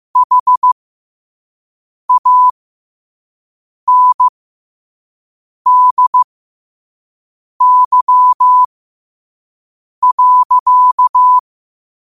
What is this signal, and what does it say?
Morse code, heard as a keyed tone, about 15 wpm, spaced as for 5 wpm, 1000 Hz. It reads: HANDY.
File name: qotdfiles / 5wpm